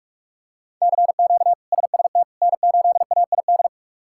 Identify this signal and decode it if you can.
Morse code, heard as a keyed tone, about 41 wpm, 700 Hz. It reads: CQ SST N8AID
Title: K1USN SST QSO Practice - 45wpm